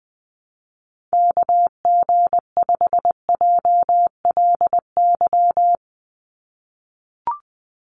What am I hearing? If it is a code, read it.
Morse code, heard as a keyed tone, about 20 wpm, 700 Hz. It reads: KG5JLY
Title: USA Call Signs 20wpm